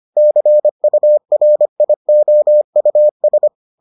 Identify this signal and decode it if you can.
Morse code, heard as a keyed tone, about 25 wpm, 600 Hz. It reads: CURIOUS